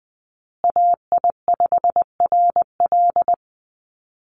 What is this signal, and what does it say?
Morse code, heard as a keyed tone, about 20 wpm, 700 Hz. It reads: AI5RL
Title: USA Call Signs Repeated 2x 20wpm